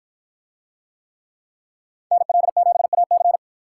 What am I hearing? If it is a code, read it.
Morse code, heard as a keyed tone, about 46 wpm, 700 Hz. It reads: NL7AC